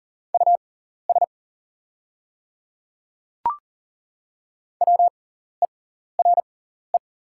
Morse code, heard as a keyed tone, about 40 wpm, spaced as for 14 wpm, 690 Hz. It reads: US WERE
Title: 6x Character Spacing - Top 300 Words - 40wpm